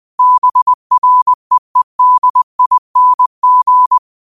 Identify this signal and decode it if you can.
Morse code, heard as a keyed tone, about 20 wpm, 1000 Hz. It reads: BREEDING
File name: 20wpm